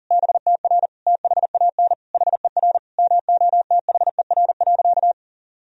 Morse code, heard as a keyed tone, about 40 wpm, 700 Hz. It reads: BTR THAN HER MOTHER.
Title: Sentences from Top 400 Words and Abbreviations 40wpm